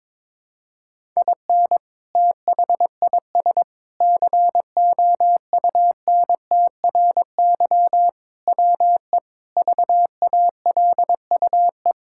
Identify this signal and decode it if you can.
Morse code, heard as a keyed tone, about 22 wpm, 700 Hz. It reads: IN THIS COUNTRY WE VALUE
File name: practice / 22wpm / Sentences with 3-Word Phrases - 22wpm